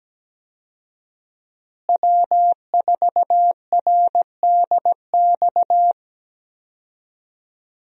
Morse code, heard as a keyed tone, about 17 wpm, 700 Hz. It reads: W4RDX
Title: USA Call Signs - 5 Characters - Repeated 2x - 17wpm